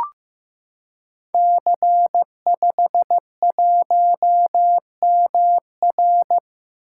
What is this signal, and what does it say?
Morse code, heard as a keyed tone, about 15 wpm, 700 Hz. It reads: C51MR
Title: Code Groups - 5 Letter:Numbers - 15wpm